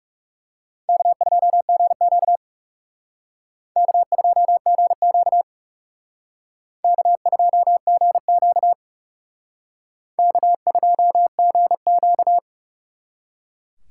Morse code, heard as a keyed tone, about 38 wpm, 700 Hz. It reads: K2GQ K2GQ K2GQ K2GQ